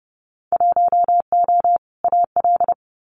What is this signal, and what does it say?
Morse code, heard as a keyed tone, about 31 wpm, 700 Hz. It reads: 1O AL
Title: Winter Field Day Exchanges Encore 30wpm